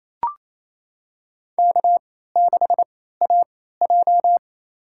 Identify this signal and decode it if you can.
Morse code, heard as a keyed tone, about 28 wpm, spaced as for 16 wpm, 700 Hz. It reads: K6AJ
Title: USA Call Signs - 4 Characters - 3x Character Spacing - 28wpm